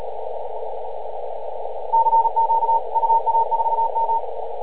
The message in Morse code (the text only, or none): CQ UA4M